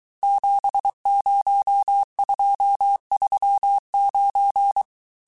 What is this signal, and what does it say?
Morse code, heard as a keyed tone, about 23 wpm, 800 Hz. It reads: 70239
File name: cours-du-mercredi / audios / cours-855-texte-chiffre-vitesse-22-mpm